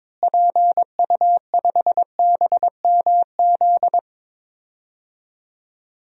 Morse code, heard as a keyed tone, about 22 wpm, 700 Hz. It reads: PU5BMZ
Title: DXCC-100 Call Signs - 6 Characters - Repeated 3x - 22wpm